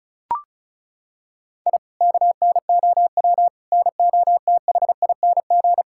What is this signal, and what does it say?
Morse code, heard as a keyed tone, about 36 wpm, 700 Hz. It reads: I KNOW NOTHING
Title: Sentences from Top 1000 Words - Review Part 6 - 35wpm